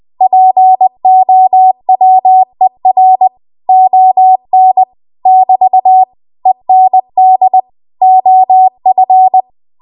Morse code, heard as a keyed tone, about 20 wpm, 750 Hz. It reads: POWER ON <BT> END OF
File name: Archive / 20 WPM / 20WPM